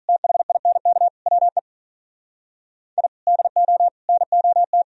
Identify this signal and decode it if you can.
Morse code, heard as a keyed tone, about 46 wpm, 700 Hz. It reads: THINK WE I DO NOT